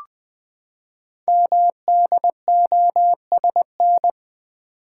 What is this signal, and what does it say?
Morse code, heard as a keyed tone, about 20 wpm, 700 Hz. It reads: MDOSN
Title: Code Groups - 5 Letter:Numbers - 20wpm